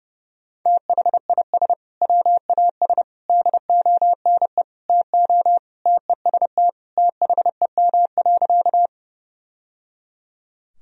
Morse code, heard as a keyed tone, about 32 wpm, 700 Hz. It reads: THIS WAS DONE TO TEST THEM.
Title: Sentences from Top 1000 Words - Review Part 3 - 30wpm